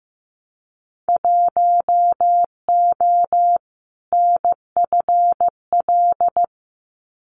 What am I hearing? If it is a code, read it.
Morse code, heard as a keyed tone, about 15 wpm, 700 Hz. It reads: 1O NFL
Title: Winter Field Day Exchanges 15wpm